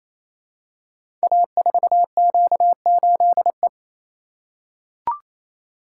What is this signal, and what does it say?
Morse code, heard as a keyed tone, about 27 wpm, 700 Hz. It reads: A4Q8E
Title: Code Groups - 5 Letter:Numbers - 28wpm